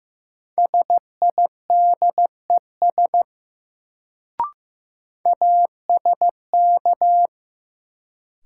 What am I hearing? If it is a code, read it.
Morse code, heard as a keyed tone, about 15 wpm, 700 Hz. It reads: SIDES ASK
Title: Sets of 4 Words - New Word Drill - 15wpm